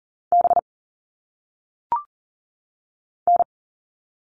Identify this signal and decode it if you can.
Morse code, heard as a keyed tone, about 40 wpm, 700 Hz. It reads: B N
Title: All Cut Numbers - 1-Digit 40wpm